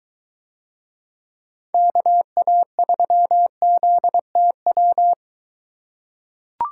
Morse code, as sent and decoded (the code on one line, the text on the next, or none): -.- .- ...-- --.. - .--
KA3ZTW